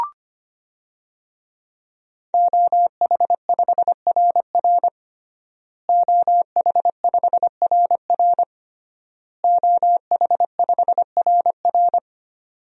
Morse code, heard as a keyed tone, about 25 wpm, 700 Hz. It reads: OH5RR OH5RR OH5RR
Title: DXCC-100 Call Signs - 5 Characters - Repeated 3x - 25wpm